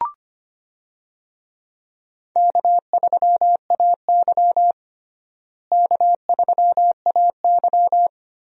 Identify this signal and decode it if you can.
Morse code, heard as a keyed tone, about 25 wpm, 700 Hz. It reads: K3AY K3AY
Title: USA Call Signs - 4 Characters - Repeated 2x - 25wpm